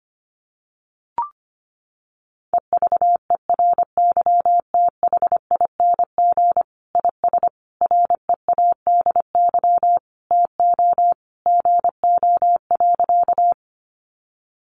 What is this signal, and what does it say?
Morse code, heard as a keyed tone, about 25 wpm, 700 Hz. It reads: EVERYTHING IS READY TO GO.
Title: Sentences from Top 700 Words 25wpm